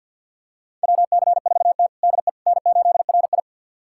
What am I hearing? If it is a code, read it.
Morse code, heard as a keyed tone, about 51 wpm, 700 Hz. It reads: WC4T DE N8RI
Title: USA Callsign DE Callsign - 4 Characters - 50wpm